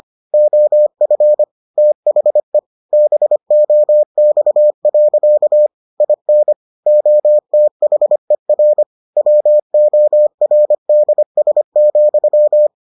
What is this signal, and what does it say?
Morse code, heard as a keyed tone, about 25 wpm, 600 Hz. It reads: OF THE BOX. IN OTHER WORDS,